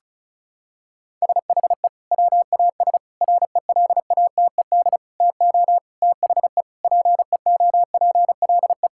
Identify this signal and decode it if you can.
Morse code, heard as a keyed tone, about 36 wpm, 700 Hz. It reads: SHE WAS RELATED TO THE PEOPLE
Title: Sentences from Top 1000 Words - Review Part 5 - 35wpm